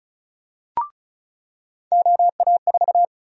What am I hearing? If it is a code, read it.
Morse code, heard as a keyed tone, about 34 wpm, 700 Hz. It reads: OA4
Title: USA Call Sign Prefixes - 35wpm